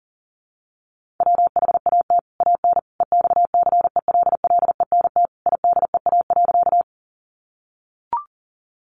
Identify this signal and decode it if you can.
Morse code, heard as a keyed tone, about 40 wpm, 700 Hz. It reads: WHAT AN EXCELLENT IDEA.
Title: Sentences from Top 1000 Words 40wpm